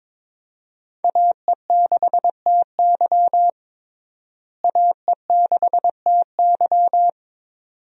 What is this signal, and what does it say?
Morse code, heard as a keyed tone, about 22 wpm, 700 Hz. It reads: AE6TY AE6TY
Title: USA Call Signs - 5 Characters - Repeated 2x - 22wpm